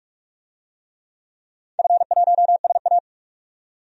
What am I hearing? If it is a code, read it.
Morse code, heard as a keyed tone, about 46 wpm, 700 Hz. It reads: F1SA
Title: DXCC-100 Call Signs - 4 Characters - 45wpm